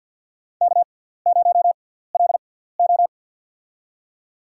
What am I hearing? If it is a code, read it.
Morse code, heard as a keyed tone, about 50 wpm, spaced as for 17 wpm, 700 Hz. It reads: K0LO
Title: USA Call Signs - 6x Character Spacing - 50wpm